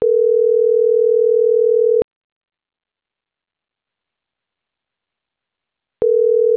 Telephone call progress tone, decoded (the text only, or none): ringback tone